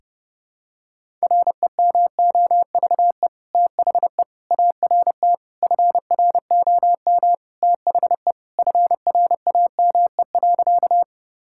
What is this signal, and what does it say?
Morse code, heard as a keyed tone, about 29 wpm, 700 Hz. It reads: REMOVE THE ART FROM THE FRAME.
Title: Sentences from Top 1000 Words - Review Part 6 - 30wpm